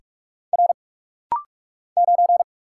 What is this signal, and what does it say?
Morse code, heard as a keyed tone, about 44 wpm, 700 Hz. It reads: R 9
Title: Single Letter-Number - Rapid-Fire 45wpm